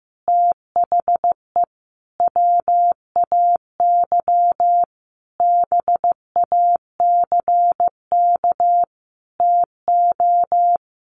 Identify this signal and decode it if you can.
Morse code, heard as a keyed tone, about 15 wpm, 700 Hz. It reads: THE WAY BACK TO
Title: A Sound Of Thunder 15wpm